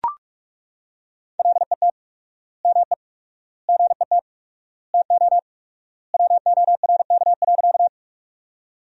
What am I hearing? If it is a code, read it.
Morse code, heard as a keyed tone, about 41 wpm, 700 Hz. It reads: LET ME GET TO WORK.
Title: 4x Word-Spacing - Sentences from Top 300 Words - 45wpm